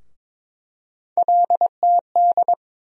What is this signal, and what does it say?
Morse code, heard as a keyed tone, about 22 wpm, 700 Hz. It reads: LTD
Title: USA Call Sign Postfixes - 3 Letters 22wpm